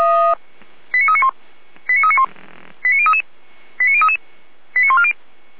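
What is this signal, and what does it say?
Five-tone selective call (ZVEI), heard as quick runs of five tones, tones of about 70 ms. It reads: 89391 89391 89033 89033 89140
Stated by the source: ZVEI five-tone selective call received on SDR.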